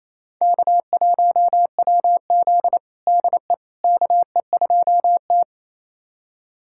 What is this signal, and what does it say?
Morse code, heard as a keyed tone, about 28 wpm, 700 Hz. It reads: K1WZ DE KE2T
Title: USA Callsign DE Callsign - 4 Characters - 28wpm